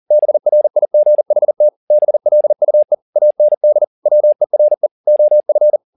Morse code, heard as a keyed tone, about 40 wpm, 600 Hz. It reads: BRIGHT BLUE AND WERE OF
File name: the old man and the sea 2 40wpm